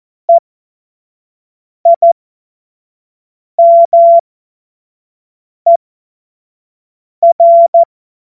Morse code, heard as a keyed tone, about 14 wpm, spaced as for 5 wpm, 680 Hz. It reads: EIMER
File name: Morsekurs / Witze / Witz 17 Juni 2025 25 F70